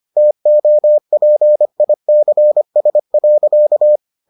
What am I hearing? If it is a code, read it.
Morse code, heard as a keyed tone, about 25 wpm, 600 Hz. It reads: TOPICS.